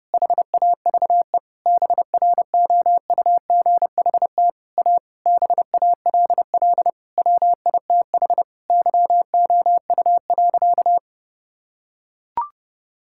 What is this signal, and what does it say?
Morse code, heard as a keyed tone, about 32 wpm, 700 Hz. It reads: HAVE BROUGHT A BALL WITH YOU.
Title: Sentences from Top 1000 Words - Review Part 6 - 30wpm